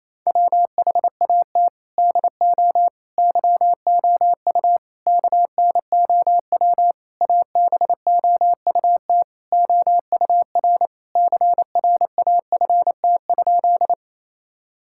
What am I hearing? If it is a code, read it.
Morse code, heard as a keyed tone, about 27 wpm, 700 Hz. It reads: WHAT DO YOU KNOW ABOUT OUR CRAFT?
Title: Sentences from Top 1000 Words - Review Part 6 - 28wpm